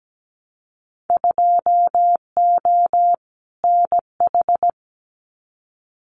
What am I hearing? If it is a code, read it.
Morse code, heard as a keyed tone, about 17 wpm, 700 Hz. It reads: 2O NH
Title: Winter Field Day Exchanges 17wpm